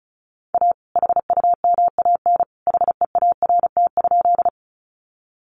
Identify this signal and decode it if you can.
Morse code, heard as a keyed tone, about 34 wpm, 700 Hz. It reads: A HUMAN HEART?
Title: Sentences from Top 1200 Words 35wpm